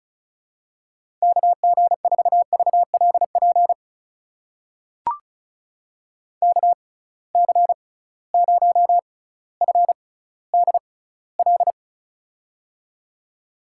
Morse code, heard as a keyed tone, about 34 wpm, spaced as for 18 wpm, 700 Hz. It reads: KG4VLP KC0FDL
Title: USA Call Signs - 6 Characters - 6x Character Spacing - 35wpm